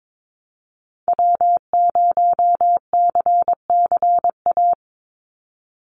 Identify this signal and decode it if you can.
Morse code, heard as a keyed tone, about 22 wpm, 700 Hz. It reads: W0CCA
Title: NA SOTA Activator Callsigns 22wpm